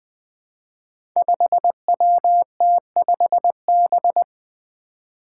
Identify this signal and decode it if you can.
Morse code, heard as a keyed tone, about 20 wpm, 700 Hz. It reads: 5WT5B
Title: Five Challenging Letter-Number Combinations 20wpm